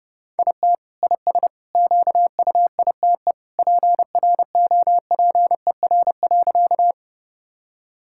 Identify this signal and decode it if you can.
Morse code, heard as a keyed tone, about 29 wpm, 700 Hz. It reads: IT IS QUITE PROPER.
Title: Sentences from Top 1000 Words - Review Part 5 - 30wpm